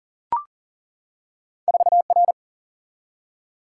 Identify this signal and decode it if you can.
Morse code, heard as a keyed tone, about 40 wpm, 700 Hz. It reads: 4 R
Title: Single-Letter Single-Number 40wpm